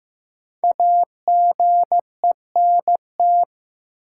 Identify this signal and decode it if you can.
Morse code, heard as a keyed tone, about 15 wpm, 700 Hz. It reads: AGENT
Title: Sentences from Top 1600 Words - New Word Drill - 15wpm